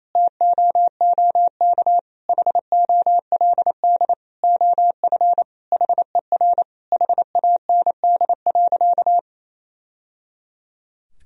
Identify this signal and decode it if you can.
Morse code, heard as a keyed tone, about 28 wpm, 700 Hz. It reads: TOOK HOLD OF HER HAND.
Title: Sentences from Top 1000 Words - Review Part 2 - 28wpm